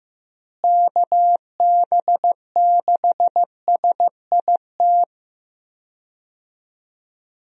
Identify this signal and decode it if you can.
Morse code, heard as a keyed tone, about 15 wpm, 700 Hz. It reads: KB6SIT
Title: USA Call Signs - 6 Characters - Repeated 3x - 15wpm